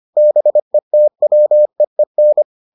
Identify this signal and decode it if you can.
Morse code, heard as a keyed tone, about 25 wpm, 600 Hz. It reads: BETWEEN